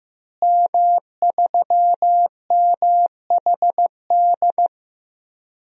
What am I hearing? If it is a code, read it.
Morse code, heard as a keyed tone, about 15 wpm, 700 Hz. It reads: M3MHD
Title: DXCC-100 Call Signs - 5 Characters - Repeated 3x - 15wpm